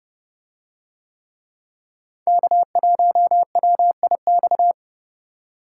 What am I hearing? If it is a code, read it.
Morse code, heard as a keyed tone, about 28 wpm, 700 Hz. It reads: K1WIX